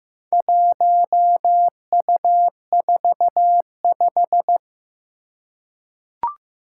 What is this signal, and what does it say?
Morse code, heard as a keyed tone, about 16 wpm, 700 Hz. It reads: 1U45
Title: Four Challenging Letter-Number Combinations 15wpm